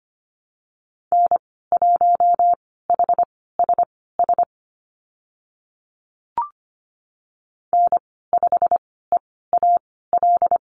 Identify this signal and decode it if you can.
Morse code, heard as a keyed tone, about 25 wpm, spaced as for 16 wpm, 700 Hz. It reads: N1HSS N5EAL
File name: Lesson 15g - Number 5 - Call Signs 25wpm